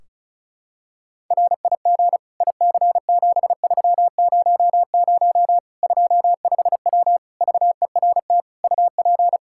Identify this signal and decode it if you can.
Morse code, heard as a keyed tone, about 36 wpm, 700 Hz. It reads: RIG IC7300 25W VERT UP
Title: Practice QSO - Exchange 2 - Info - 35wpm